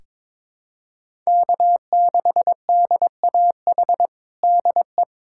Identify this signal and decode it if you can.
Morse code, heard as a keyed tone, about 22 wpm, 700 Hz. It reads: K6DAH DE
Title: USA Callsign DE Callsign - 5 Characters - 22wpm